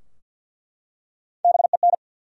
Morse code, heard as a keyed tone, about 49 wpm, 700 Hz. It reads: BEN
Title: USA Common Names - Max 5 Letters - 50wpm